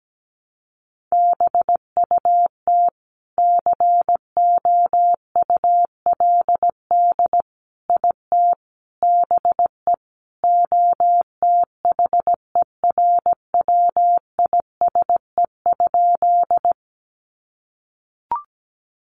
Morse code, heard as a keyed tone, about 17 wpm, 700 Hz. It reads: BUT COULD IT BE OTHERWISE?
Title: Sentences from Top 1200 Words 17wpm